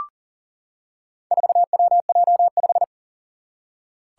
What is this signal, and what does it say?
Morse code, heard as a keyed tone, about 40 wpm, 700 Hz. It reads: 4WJ5
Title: Four Challenging Letter-Number Combinations 40wpm